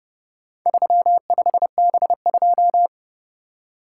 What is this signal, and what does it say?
Morse code, heard as a keyed tone, about 29 wpm, 700 Hz. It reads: 35B2